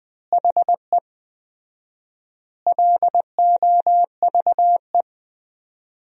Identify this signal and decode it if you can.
Morse code, heard as a keyed tone, about 20 wpm, 700 Hz. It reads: HE LOVE
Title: 4x Word-Spacing - Sentences from Top 400 Words - 20wpm